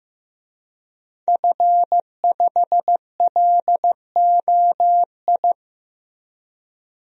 Morse code, heard as a keyed tone, about 15 wpm, 700 Hz. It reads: F5LOI
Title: DXCC-100 Call Signs - 5 Characters - Repeated 3x - 15wpm